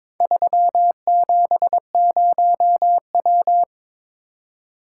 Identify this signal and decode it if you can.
Morse code, heard as a keyed tone, about 22 wpm, 700 Hz. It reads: 370W
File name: practice / 22wpm / Four Letter-Number Combinations - 22wpm